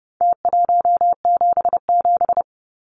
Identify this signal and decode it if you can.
Morse code, heard as a keyed tone, about 30 wpm, 700 Hz. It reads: T177
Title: Numbers - Contest Style - 4-Digit 30wpm